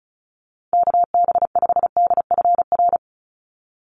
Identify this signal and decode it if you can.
Morse code, heard as a keyed tone, about 34 wpm, 700 Hz. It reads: KB5DFR
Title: USA Call Signs 35wpm